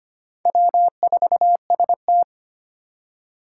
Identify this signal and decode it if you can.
Morse code, heard as a keyed tone, about 25 wpm, 700 Hz. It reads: W4ST
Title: Four Challenging Letter-Number Combinations 25wpm